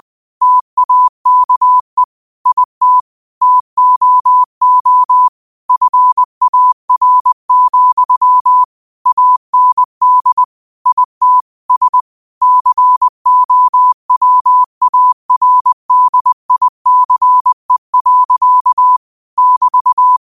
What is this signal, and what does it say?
Morse code, heard as a keyed tone, about 20 wpm, 1000 Hz. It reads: TAKE IT TOO FAR, AND IT S COWARDICE. <BT>
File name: qotdfiles / 20wpm